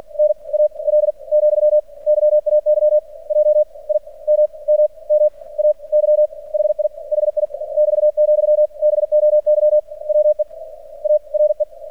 Morse code, heard as a keyed tone, about 49 wpm, spaced as for 34 wpm, 600 Hz. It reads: NAL, YAY JIMMMAY HI HI <BT>, BOY WE ARE
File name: archived / CRSnet